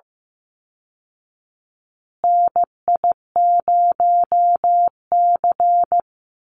Morse code, heard as a keyed tone, about 15 wpm, 700 Hz. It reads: NI0C
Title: Call Signs Repeated 3x - 15wpm